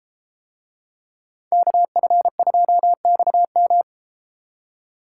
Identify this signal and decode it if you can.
Morse code, heard as a keyed tone, about 33 wpm, 700 Hz. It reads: KF2XM